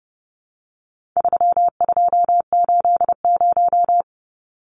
Morse code, heard as a keyed tone, about 30 wpm, 700 Hz. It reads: 3280
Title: Numbers - 4-Digit 30wpm